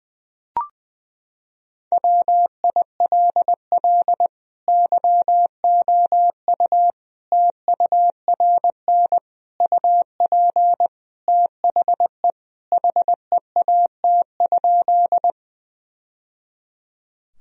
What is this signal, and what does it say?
Morse code, heard as a keyed tone, about 20 wpm, 700 Hz. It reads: WILL YOU TURN UP THE HEAT?
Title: Sentences from Top 1000 Words - Review Part 3 - 20wpm